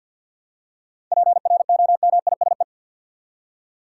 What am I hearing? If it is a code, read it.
Morse code, heard as a keyed tone, about 51 wpm, 700 Hz. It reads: PROMISE
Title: Seven Letter Words - 50wpm